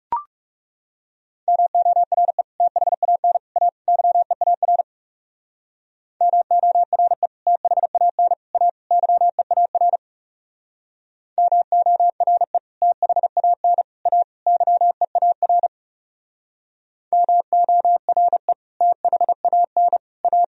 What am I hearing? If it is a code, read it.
Morse code, heard as a keyed tone, about 37 wpm, 700 Hz. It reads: MORE THAN A YEAR MORE THAN A YEAR MORE THAN A YEAR MORE THAN A
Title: Speed-Racing - Sets of 4 Words 30wpm